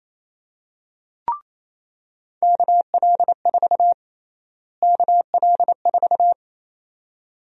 Morse code, heard as a keyed tone, about 28 wpm, 700 Hz. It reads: KL4 KL4
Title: USA Call Sign Prefixes - Repeated 2x - 28wpm